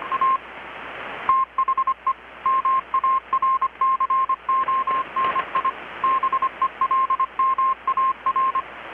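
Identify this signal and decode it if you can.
Morse code, heard as a keyed tone, about 25 wpm, 1060 Hz. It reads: A THE MARCONI BELMAR
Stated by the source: Morse from KPH's Night of Nights XX maritime radiotelegraphy event, received in Chiapas, Mexico about 3,800 km away on a Kenwood R-600 with a grounded 10 m long-wire antenna.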